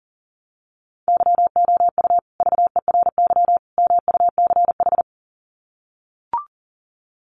Morse code, heard as a keyed tone, about 37 wpm, 700 Hz. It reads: YOU VERY MUCH
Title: Sets of 3 Words 40wpm